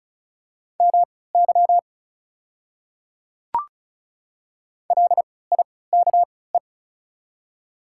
Morse code, heard as a keyed tone, about 36 wpm, spaced as for 20 wpm, 700 Hz. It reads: MY LIKE
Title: 3x Character Spacing - Top 100 Words - 35wpm